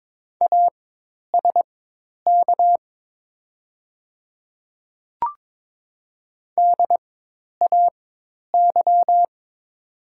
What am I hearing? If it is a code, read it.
Morse code, heard as a keyed tone, about 22 wpm, spaced as for 10 wpm, 700 Hz. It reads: ASK DAY
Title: 4x Character Spacing - Top 200 Words - 22wpm